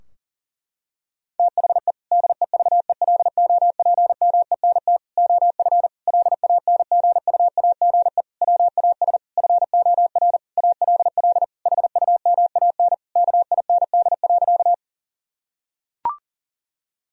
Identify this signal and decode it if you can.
Morse code, heard as a keyed tone, about 40 wpm, 700 Hz. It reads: THE DEVELOPMENT OF LANGUAGE WAS FOR ALL HUMAN KIND.
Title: Sentences from Top 1000 Words - Review Part 4 - 40wpm